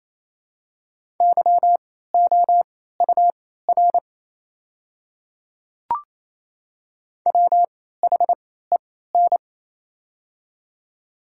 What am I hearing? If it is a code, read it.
Morse code, heard as a keyed tone, about 27 wpm, spaced as for 16 wpm, 690 Hz. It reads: YOUR WHEN
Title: 3x Character Spacing - Top 100 Words - 28wpm